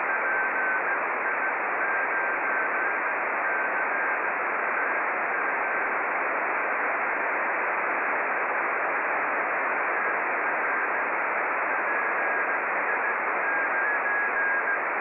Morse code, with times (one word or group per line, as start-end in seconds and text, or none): none